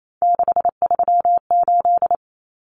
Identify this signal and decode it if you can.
Morse code, heard as a keyed tone, about 28 wpm, 700 Hz. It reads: 638
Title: Numbers - 3-Digit 28wpm